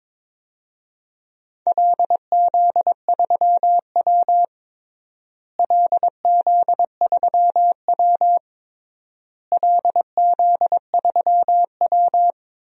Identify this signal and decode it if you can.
Morse code, heard as a keyed tone, about 22 wpm, 700 Hz. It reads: LZ3W LZ3W LZ3W
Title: DXCC-100 Call Signs - 4 Characters - Repeated 3x - 22wpm